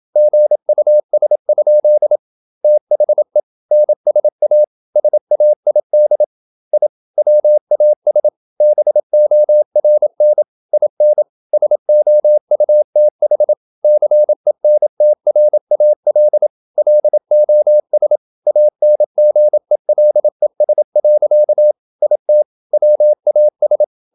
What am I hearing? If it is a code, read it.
Morse code, heard as a keyed tone, about 27 wpm, 600 Hz. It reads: GUS? THE NSA SAID I WAS BORN IN SOUTH CENTRAL LOS ANGELES. IT WAS
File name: Never 0027wpm